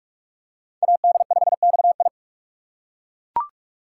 Morse code, heard as a keyed tone, about 40 wpm, 700 Hz. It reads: AD5XI